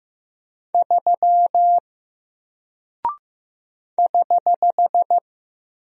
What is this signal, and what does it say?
Morse code, heard as a keyed tone, about 15 wpm, 700 Hz. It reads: 3 <HH>
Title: Single Character (Letter:Number:Punctuation:Prosign) Alt 15wpm